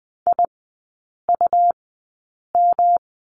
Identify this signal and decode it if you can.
Morse code, heard as a keyed tone, about 20 wpm, spaced as for 8 wpm, 700 Hz. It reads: IUM
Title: 107 Lesson 18e - Letter m - Three Characters 20wpm